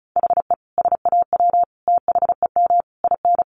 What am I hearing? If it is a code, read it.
Morse code, heard as a keyed tone, about 34 wpm, 700 Hz. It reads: HE SAW THEM IN
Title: Sentences from Top 1300 Words 35wpm